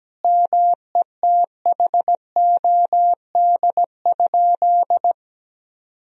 Morse code, heard as a keyed tone, about 17 wpm, 700 Hz. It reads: METHOD?
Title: Sentences from Top 1000 Words - Review Part 3 - 17wpm